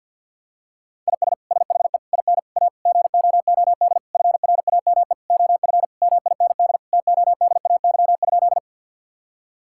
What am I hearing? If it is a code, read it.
Morse code, heard as a keyed tone, about 51 wpm, 700 Hz. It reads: IS SHE IN A GOOD FRAME OF MIND TODAY?
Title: Sentences from Top 1000 Words - Review Part 6 - 50wpm